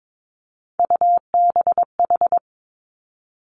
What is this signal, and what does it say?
Morse code, heard as a keyed tone, about 22 wpm, 700 Hz. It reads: UBH
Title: Three Challenging Letter-Number Combinations 22wpm